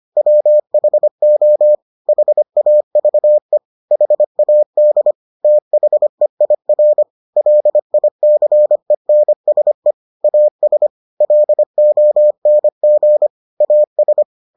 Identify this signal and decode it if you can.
Morse code, heard as a keyed tone, about 25 wpm, 600 Hz. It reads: WHO HAVE HAD THEIR LICENSE AS LONG AS